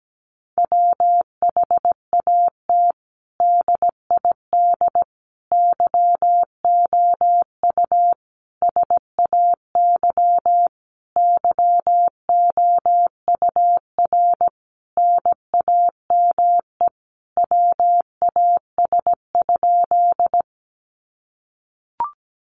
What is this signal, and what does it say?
Morse code, heard as a keyed tone, about 17 wpm, 700 Hz. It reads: WHAT DID YOU SAY YOUR NAME WAS?
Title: Sentences from Top 200 Words 17wpm